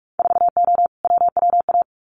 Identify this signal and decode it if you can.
Morse code, heard as a keyed tone, about 46 wpm, 700 Hz. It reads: 4O WWA
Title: Winter Field Day Exchanges 45wpm